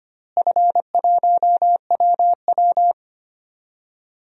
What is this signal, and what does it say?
Morse code, heard as a keyed tone, about 25 wpm, 700 Hz. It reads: F1WW
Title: DXCC-100 Call Signs - 4 Characters - Repeated 3x - 25wpm